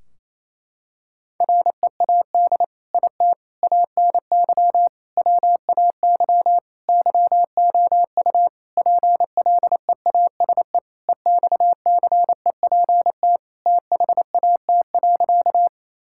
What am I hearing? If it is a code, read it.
Morse code, heard as a keyed tone, about 28 wpm, 700 Hz. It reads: READ IT ANY WAY YOU PLEASE EXCEPT THAT.
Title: Sentences from Top 1600 Words 28wpm